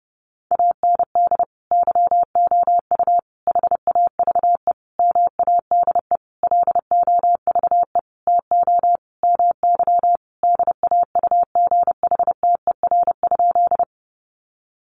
Morse code, heard as a keyed tone, about 30 wpm, 700 Hz. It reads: AND YOU HAVE MADE LOVE TO MY DAUGHTER?
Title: Sentences from Top 1500 Words 30wpm